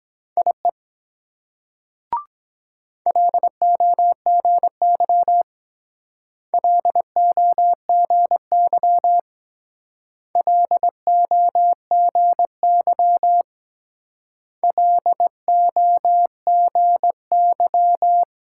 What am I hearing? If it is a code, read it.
Morse code, heard as a keyed tone, about 22 wpm, 700 Hz. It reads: IE LOGY LOGY LOGY LOGY